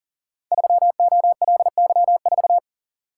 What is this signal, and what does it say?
Morse code, heard as a keyed tone, about 42 wpm, 690 Hz. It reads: 3OLY4
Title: Code Groups - 5 Letter:Numbers - 40wpm